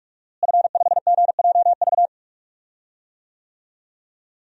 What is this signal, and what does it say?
Morse code, heard as a keyed tone, about 46 wpm, 700 Hz. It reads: F5GJV
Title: DXCC-100 Call Signs - 5 Characters - Repeated 3x - 45wpm